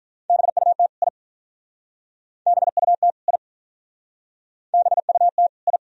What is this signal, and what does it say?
Morse code, heard as a keyed tone, about 46 wpm, 700 Hz. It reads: BUT I BUT I BUT I